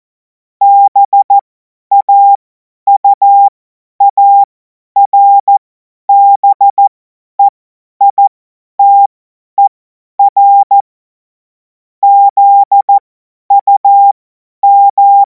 Morse code, heard as a keyed tone, about 14 wpm, spaced as for 10 wpm, 800 Hz. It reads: BAUARBEITER ZUM
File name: Morsekurs / Witze / Witz 31. März 2026 50 F70